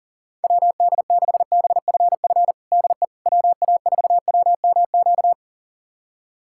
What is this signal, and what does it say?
Morse code, heard as a keyed tone, about 40 wpm, 700 Hz. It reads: WD6BFF DE WA4WMQ
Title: USA Callsign DE Callsign - 6 Characters - 40wpm